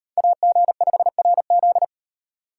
Morse code, heard as a keyed tone, about 38 wpm, 700 Hz. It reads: AG5RZ